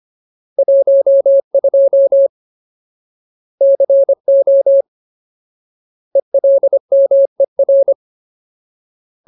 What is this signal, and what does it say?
Morse code, heard as a keyed tone, about 25 wpm, 550 Hz. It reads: 12 CO ELMER
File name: Ultimate Word List / 25WPM Ultimate